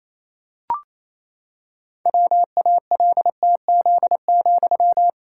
Morse code, heard as a keyed tone, about 28 wpm, 700 Hz. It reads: WALTZ,
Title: Pangram Challenge v2 - 28wpm